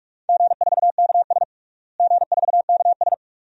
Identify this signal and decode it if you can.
Morse code, heard as a keyed tone, about 46 wpm, 700 Hz. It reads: G4KS G4KS
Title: DXCC-100 Call Signs - 4 Characters - Repeated 2x - 45wpm